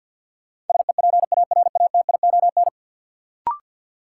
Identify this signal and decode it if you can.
Morse code, heard as a keyed tone, about 47 wpm, 700 Hz. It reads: SEPARATION
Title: Sets of 5 Words - New Word Drill - 50wpm